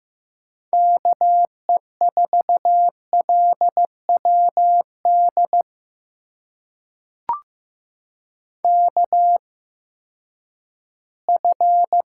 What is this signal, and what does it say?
Morse code, heard as a keyed tone, about 15 wpm, spaced as for 10 wpm, 700 Hz. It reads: KE4LWD KF
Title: USA Call Signs - 6 Characters - 8x Character Spacing - 15wpm